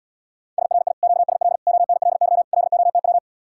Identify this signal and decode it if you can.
Morse code, heard as a keyed tone, about 77 wpm, 700 Hz. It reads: SRI QSL ONLY LOTW